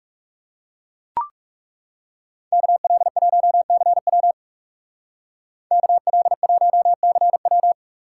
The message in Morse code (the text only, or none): KL1CW KL1CW